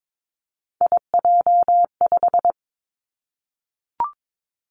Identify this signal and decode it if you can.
Morse code, heard as a keyed tone, about 22 wpm, 700 Hz. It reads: IJ5
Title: Three Challenging Letter-Number Combinations 22wpm